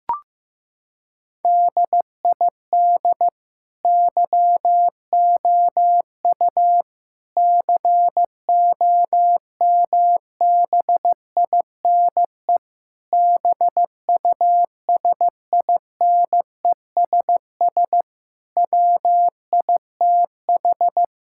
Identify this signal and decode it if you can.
Morse code, heard as a keyed tone, about 15 wpm, 700 Hz. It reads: DID YOU COMBINE BUSINESS WITH
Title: Sentences from Top 1600 Words 15wpm